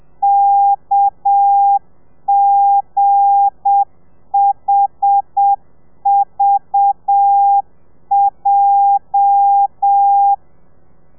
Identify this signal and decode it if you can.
Morse code, heard as a keyed tone, about 7 wpm, 800 Hz. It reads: KGHVJ
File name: telegrafia / i2tfj / Lez8 30